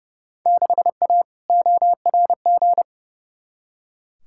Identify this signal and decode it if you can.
Morse code, heard as a keyed tone, about 29 wpm, 700 Hz. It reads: 6A ORG
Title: Field Day Exchanges Encore - 30wpm